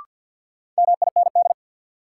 Morse code, heard as a keyed tone, about 53 wpm, 700 Hz. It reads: MIND
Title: Sentences from Top 500 Words - New Word Drill - 50wpm